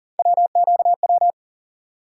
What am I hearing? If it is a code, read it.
Morse code, heard as a keyed tone, about 40 wpm, 700 Hz. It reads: WQW
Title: USA Call Sign Postfixes - 3 Letters 40wpm